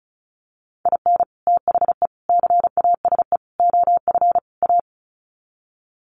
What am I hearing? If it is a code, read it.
Morse code, heard as a keyed tone, about 34 wpm, 700 Hz. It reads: IN THE CASE OF A
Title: Sets of 5 Words Encore 35wpm